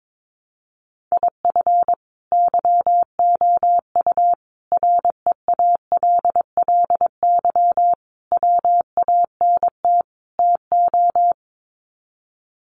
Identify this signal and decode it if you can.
Morse code, heard as a keyed tone, about 22 wpm, 700 Hz. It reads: IF YOU REALLY WANT TO
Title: Sets of 5 Words Encore2 22wpm